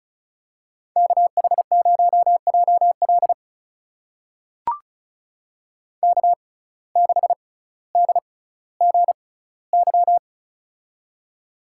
Morse code, heard as a keyed tone, about 36 wpm, spaced as for 18 wpm, 700 Hz. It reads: KH0JL K6DGY
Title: USA Call Signs - 5 Characters - 6x Character Spacing - 35wpm